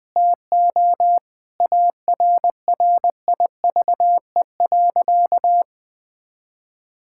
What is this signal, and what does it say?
Morse code, heard as a keyed tone, about 21 wpm, 700 Hz. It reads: TO ARRIVE.
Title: Sentences with 5-Word Phrases - 20wpm